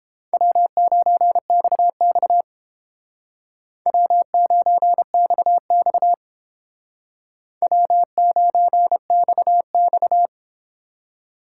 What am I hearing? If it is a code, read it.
Morse code, heard as a keyed tone, about 29 wpm, 700 Hz. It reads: W9XX W9XX W9XX